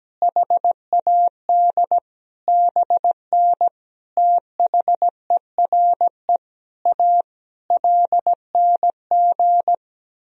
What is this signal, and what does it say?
Morse code, heard as a keyed tone, about 17 wpm, 700 Hz. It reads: HAD BN THERE A LNG
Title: Sentences from Top 400 Words and Abbreviations 17wpm